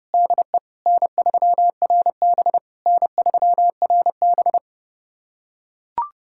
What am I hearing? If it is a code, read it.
Morse code, heard as a keyed tone, about 32 wpm, 700 Hz. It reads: DE N3RB N3RB
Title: SOTA QSO Practice 30wpm